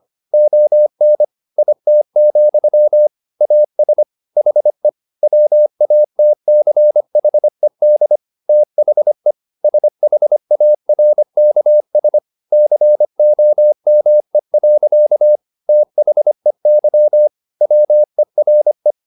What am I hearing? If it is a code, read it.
Morse code, heard as a keyed tone, about 25 wpm, 600 Hz. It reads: ON IT, AS HE WATCHED THE SHARKS COME. THEY WERE